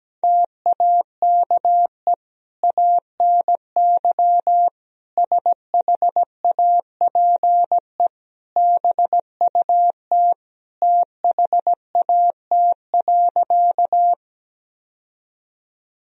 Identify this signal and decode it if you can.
Morse code, heard as a keyed tone, about 17 wpm, 700 Hz. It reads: TAKE ANY SHAPE BUT THAT.
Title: Sentences from Top 1000 Words - Review Part 3 - 17wpm